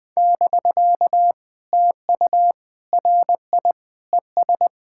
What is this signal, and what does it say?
Morse code, heard as a keyed tone, about 19 wpm, 700 Hz. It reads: <BK> TU RI ES
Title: POTA QSO Practice - 20wpm